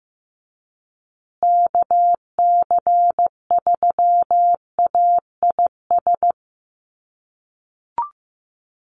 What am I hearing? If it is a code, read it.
Morse code, heard as a keyed tone, about 15 wpm, 700 Hz. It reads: KC3AIS
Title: USA Call Signs Repeated 3x 15wpm